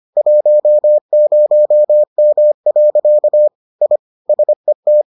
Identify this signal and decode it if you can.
Morse code, heard as a keyed tone, about 25 wpm, 600 Hz. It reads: 10M. I SET